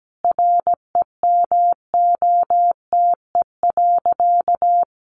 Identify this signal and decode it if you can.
Morse code, heard as a keyed tone, about 17 wpm, 700 Hz. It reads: REMOTE.
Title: Sentences from Top 1300 Words 17wpm